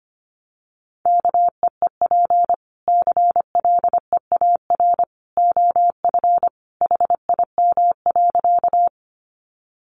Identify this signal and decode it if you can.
Morse code, heard as a keyed tone, about 25 wpm, 700 Hz. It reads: KEEP CLEAR OF HIM.
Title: Sentences from Top 500 Words 25wpm